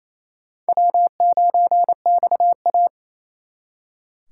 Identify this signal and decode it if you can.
Morse code, heard as a keyed tone, about 38 wpm, 700 Hz. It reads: W9XA